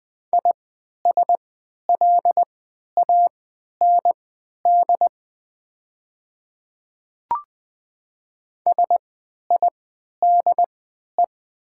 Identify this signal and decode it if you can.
Morse code, heard as a keyed tone, about 20 wpm, spaced as for 11 wpm, 700 Hz. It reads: ISLAND SIDE
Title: 3x Character Spacing - Top 500 Words - 20wpm